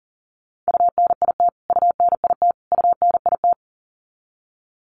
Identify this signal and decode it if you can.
Morse code, heard as a keyed tone, about 40 wpm, 700 Hz. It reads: UNIT UNIT UNIT
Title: Top 500 Words Repeated 40wpm